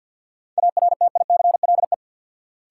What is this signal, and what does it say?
Morse code, heard as a keyed tone, about 46 wpm, 700 Hz. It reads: ARTICLE
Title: Sentences from Top 1000 Words - Review Part 5 - New Word Drill - 50wpm